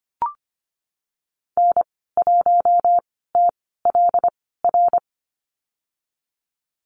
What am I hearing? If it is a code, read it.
Morse code, heard as a keyed tone, about 25 wpm, spaced as for 16 wpm, 700 Hz. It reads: N1TLR
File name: Lesson 15g - Number 5 - Call Signs 25wpm